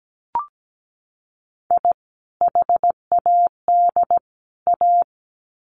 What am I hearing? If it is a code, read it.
Morse code, heard as a keyed tone, about 17 wpm, 700 Hz. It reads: I HAD A
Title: Sets of 3 Words 17wpm